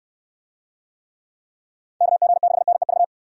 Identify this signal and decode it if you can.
Morse code, heard as a keyed tone, about 67 wpm, 700 Hz. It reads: KG6NEV